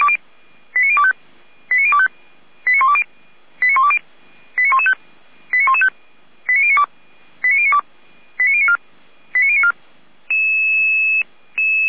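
Five-tone selective call (ZVEI), heard as quick runs of five tones, tones of about 70 ms. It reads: 89025 89025 89120 89115 89115 89092 89092 89094 89094
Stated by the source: ZVEI five-tone selective call received by SDR.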